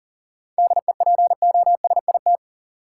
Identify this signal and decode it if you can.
Morse code, heard as a keyed tone, about 40 wpm, 700 Hz. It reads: DEPOSIT
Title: Sentences from Top 1300 Words - New Word Drill Alt - 40wpm